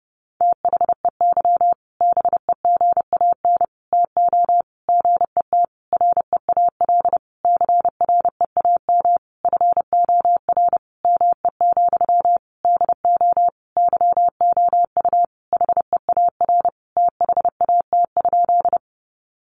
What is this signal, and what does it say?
Morse code, heard as a keyed tone, about 30 wpm, 700 Hz. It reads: THEY BEGAN TO GET REAL CREAM FOR ME, DO YOU HEAR THAT?
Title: Sentences from Top 1500 Words 30wpm